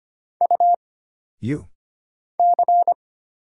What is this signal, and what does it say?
Morse code, heard as a keyed tone, about 25 wpm, 700 Hz. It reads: U C
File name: Single Character (Letter:Number:Punctuation:Prosign) - Mind-Melt - 25wpm